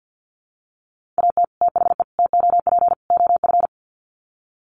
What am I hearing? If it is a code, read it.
Morse code, heard as a keyed tone, about 50 wpm, 700 Hz. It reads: AT THE TOP OF
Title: Sets of 4 Words 50wpm